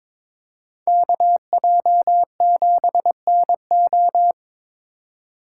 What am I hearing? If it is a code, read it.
Morse code, heard as a keyed tone, about 27 wpm, 700 Hz. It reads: KJ7NO
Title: Speed Racing - USA 2x2 Call Signs 22wpm